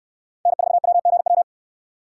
Morse code, heard as a keyed tone, about 69 wpm, 700 Hz. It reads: N4WGW